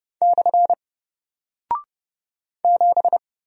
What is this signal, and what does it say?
Morse code, heard as a keyed tone, about 30 wpm, 700 Hz. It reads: / 7
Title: Single Character (Letter:Number:Punctuation:Prosign) 30wpm